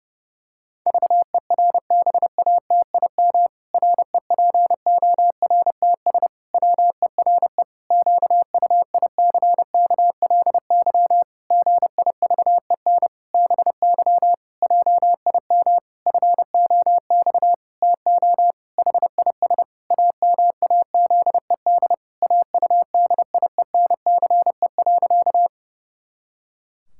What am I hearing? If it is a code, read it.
Morse code, heard as a keyed tone, about 30 wpm, 700 Hz. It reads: VERBATIM REPORTS WERE QUICKLY GIVEN BY JIM FOX TO HIS AMAZED AUDIENCE.
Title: Pangram Challenge v2 - 30wpm